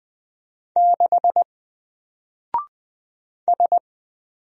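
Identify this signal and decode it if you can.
Morse code, heard as a keyed tone, about 20 wpm, 700 Hz. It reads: 6 S
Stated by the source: Single letter and number Morse drill at 20 wpm, from N4LCD.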